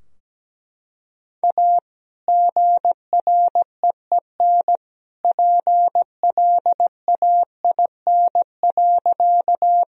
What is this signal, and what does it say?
Morse code, heard as a keyed tone, about 17 wpm, 700 Hz. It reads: A GREEN PLAIN.
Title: Sentences from Top 1000 Words - Review Part 2 - 17wpm